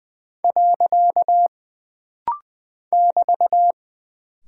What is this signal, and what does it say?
Morse code, heard as a keyed tone, about 20 wpm, 700 Hz. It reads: . <BT>
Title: Single Punctuation-Prosign - Rapid-Fire - 20wpm